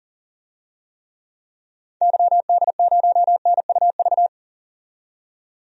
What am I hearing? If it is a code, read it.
Morse code, heard as a keyed tone, about 40 wpm, 700 Hz. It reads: YD0NUV
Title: DXCC-100 Call Signs - 6 Characters - 40wpm